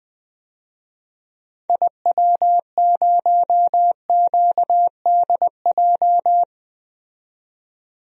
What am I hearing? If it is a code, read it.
Morse code, heard as a keyed tone, about 20 wpm, 700 Hz. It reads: IW0QDJ
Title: DXCC-100 Call Signs - 6 Characters - 20wpm